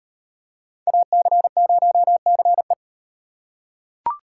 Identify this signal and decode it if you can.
Morse code, heard as a keyed tone, about 35 wpm, 700 Hz. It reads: AC0CE